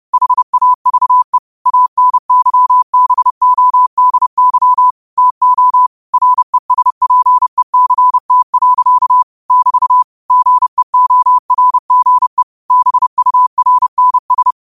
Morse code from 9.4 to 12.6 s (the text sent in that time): <BT> GEORGE